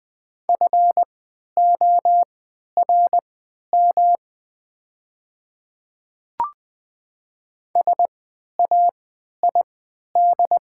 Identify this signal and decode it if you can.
Morse code, heard as a keyed tone, about 20 wpm, spaced as for 11 wpm, 700 Hz. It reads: FORM SAID
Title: 3x Character Spacing - Top 200 Words - 20wpm